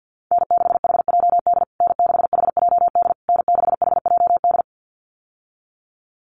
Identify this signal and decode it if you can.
Morse code, heard as a keyed tone, about 50 wpm, 700 Hz. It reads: N6HJD N6HJD N6HJD
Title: USA Call Signs Repeated 3x 50wpm